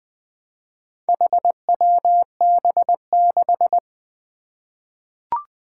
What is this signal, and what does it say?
Morse code, heard as a keyed tone, about 20 wpm, 700 Hz. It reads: HWB6
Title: Four Challenging Letter-Number Combinations 20wpm